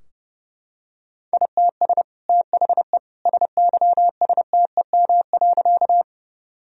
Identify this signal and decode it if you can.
Morse code, heard as a keyed tone, about 30 wpm, 700 Hz. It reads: ITS THE SYSTEM.
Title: Sentences from Top 1000 Words - Review Part 3 - 30wpm